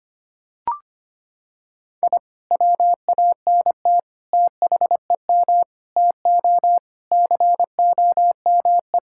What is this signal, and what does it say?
Morse code, heard as a keyed tone, about 25 wpm, 700 Hz. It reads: I WANT THEM TO COME
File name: practice / 25wpm / Sentences with 2-Word Phrases - 25wpm